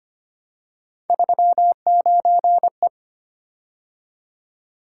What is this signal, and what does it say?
Morse code, heard as a keyed tone, about 25 wpm, 700 Hz. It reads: 39E
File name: Three Letter-Number Combinations - 25wpm